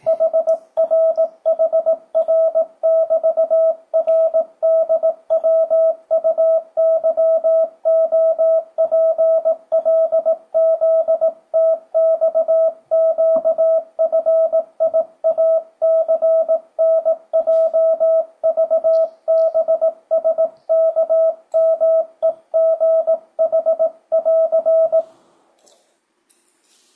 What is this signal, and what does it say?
Morse code, heard as a keyed tone, about 18 wpm, 660 Hz. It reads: HRHR<BT>RDWUYOPLZTXQFIACNJVBSKMEGH<AR>
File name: 2025cw / speedup / 18wpm